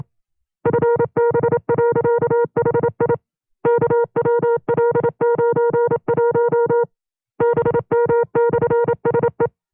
Morse code, heard as a keyed tone, about 27 wpm, 470 Hz. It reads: FB.HI KWL91 BM/SE
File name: 25WPM